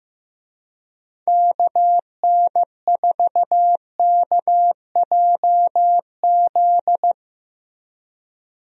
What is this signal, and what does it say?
Morse code, heard as a keyed tone, about 15 wpm, 700 Hz. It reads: KN4KJZ
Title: USA Call Signs - 6 Characters - Repeated 2x - 15wpm